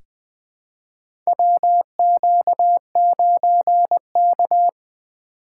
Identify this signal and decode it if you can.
Morse code, heard as a keyed tone, about 20 wpm, 700 Hz. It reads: WQ9K
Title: USA Call Signs - 4 Characters - 2x Character Spacing - 20wpm